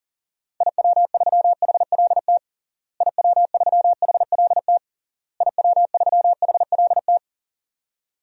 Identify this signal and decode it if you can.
Morse code, heard as a keyed tone, about 40 wpm, 700 Hz. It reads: IW3HLT IW3HLT IW3HLT
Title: DXCC-100 Call Signs - 6 Characters - Repeated 3x - 40wpm